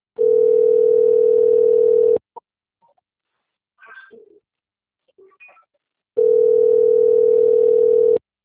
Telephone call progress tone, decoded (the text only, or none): ringback tone